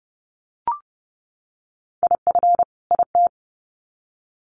Morse code, heard as a keyed tone, about 30 wpm, 700 Hz. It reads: IF IT
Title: Sets of 2 Words 30wpm